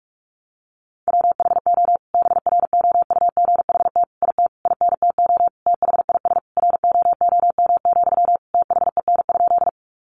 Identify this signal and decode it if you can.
Morse code, heard as a keyed tone, about 44 wpm, 700 Hz. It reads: WHO BROUGHT IT INTO THIS ROOM, THEN?
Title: Sentences from Top 600 Words 45wpm